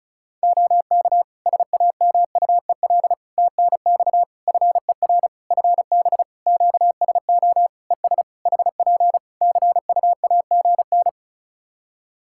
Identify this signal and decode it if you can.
Morse code, heard as a keyed tone, about 36 wpm, 700 Hz. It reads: OK SAMUEL TNX FER FB QSO ES HP CUAGN
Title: K1USN SST QSO Practice 35wpm